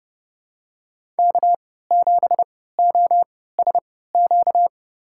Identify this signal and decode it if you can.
Morse code, heard as a keyed tone, about 30 wpm, spaced as for 17 wpm, 700 Hz. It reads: K7OSQ
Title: USA Call Signs - 5 Characters - 3x Character Spacing - 30wpm